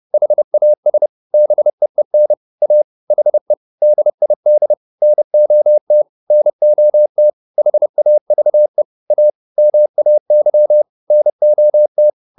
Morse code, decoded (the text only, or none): HAS BEEN A HE DID NOT NOT HAVE A MAY NOT